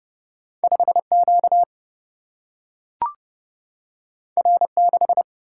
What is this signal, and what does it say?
Morse code, heard as a keyed tone, about 30 wpm, 700 Hz. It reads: 5Q R6
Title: Single-Letter Single-Number 30wpm